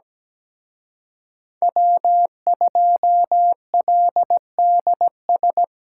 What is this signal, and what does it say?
Morse code, heard as a keyed tone, about 20 wpm, 700 Hz. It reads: W2LDS
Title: Speed Racing - USA 1x3 Call Signs 17wpm